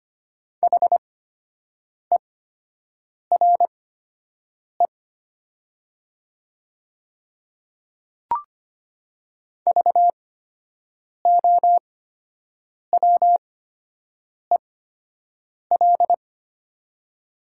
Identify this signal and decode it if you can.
Morse code, heard as a keyed tone, about 25 wpm, spaced as for 7 wpm, 700 Hz. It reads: HERE VOWEL
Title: 8x Character Spacing - Top 500 Words - 25wpm